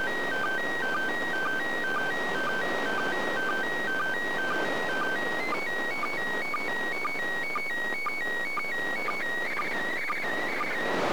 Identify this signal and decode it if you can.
SSTV (slow-scan television), PD 120.
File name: 2020.10.02 ISS